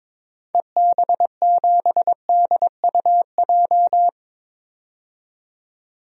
Morse code, heard as a keyed tone, about 21 wpm, 700 Hz. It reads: EB7DUJ
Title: DXCC-100 Call Signs - 6 Characters - 22wpm